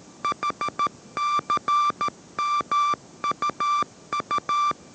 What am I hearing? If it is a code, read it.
Morse code, heard as a keyed tone, about 13 wpm, 1180 Hz. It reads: HCMUU
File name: 14wpm